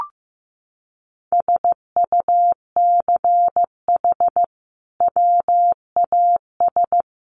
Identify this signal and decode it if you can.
Morse code, heard as a keyed tone, about 15 wpm, 700 Hz. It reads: SUCH WAS
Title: Sentences from Top 1400 Words 15wpm